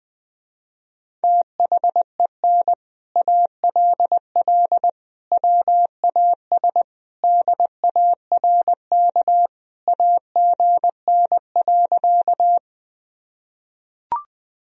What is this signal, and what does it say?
Morse code, heard as a keyed tone, about 20 wpm, 700 Hz. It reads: THEN ALL WAS DARK AGN.
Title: Sentences from Top 500 Words and Abbreviations 20wpm